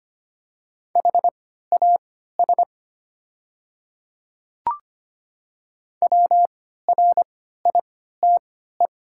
Morse code, heard as a keyed tone, about 25 wpm, spaced as for 14 wpm, 690 Hz. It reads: HAS WRITE
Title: 3x Character Spacing - Top 500 Words - 25wpm